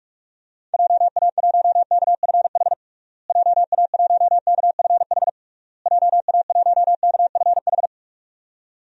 Morse code, heard as a keyed tone, about 46 wpm, 700 Hz. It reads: JA1KFH JA1KFH JA1KFH
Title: DXCC-100 Call Signs - 6 Characters - Repeated 3x - 45wpm